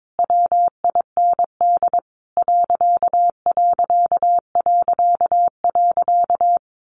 Morse code, heard as a keyed tone, about 22 wpm, 700 Hz. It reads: WIND ....
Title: A Sound Of Thunder 22wpm